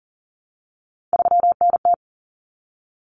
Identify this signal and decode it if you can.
Morse code, heard as a keyed tone, about 40 wpm, 700 Hz. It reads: 3NT
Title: Common Cut Numbers - 3-Digit 40wpm